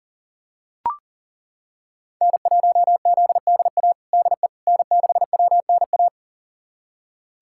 Morse code, heard as a keyed tone, about 40 wpm, 700 Hz. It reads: N1ZDA DE N6WNA
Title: USA Callsign DE Callsign - 5 Characters - 40wpm